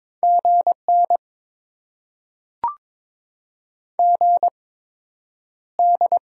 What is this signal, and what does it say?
Morse code, heard as a keyed tone, about 22 wpm, spaced as for 9 wpm, 700 Hz. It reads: GN GD
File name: Top 90 QSO Elements - 8x Character Spacing - 22wpm